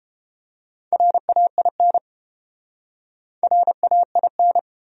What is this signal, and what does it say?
Morse code, heard as a keyed tone, about 29 wpm, 700 Hz. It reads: RAIN RAIN